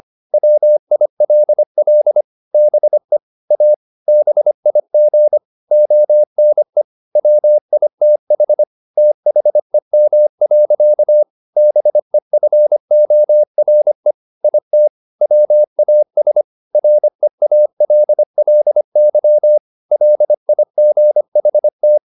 Morse code, heard as a keyed tone, about 25 wpm, 600 Hz. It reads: WILL BE A BIG ONE WITH THEM. BEFORE IT WAS REALLY LIGHT
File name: the old man and the sea part 1 25wpm